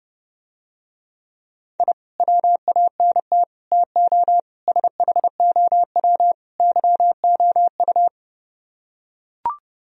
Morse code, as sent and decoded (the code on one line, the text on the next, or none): .. / .-- .- -. - / - --- / ... .... --- .-- / -.-- --- ..-
I WANT TO SHOW YOU